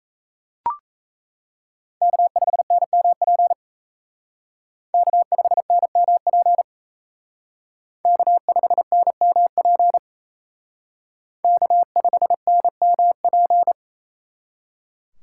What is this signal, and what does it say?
Morse code, heard as a keyed tone, about 35 wpm, 700 Hz. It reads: K5NMP K5NMP K5NMP K5NMP
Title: Speed Racing - USA 1x3 Call Signs 28wpm